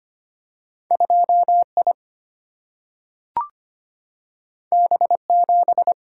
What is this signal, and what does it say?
Morse code, heard as a keyed tone, about 25 wpm, 700 Hz. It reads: 2I B7
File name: Two Letter-Number Combinations - 25wpm